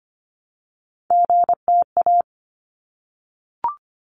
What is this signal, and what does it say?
Morse code, heard as a keyed tone, about 25 wpm, 700 Hz. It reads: GTA
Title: ARRL Sections 25wpm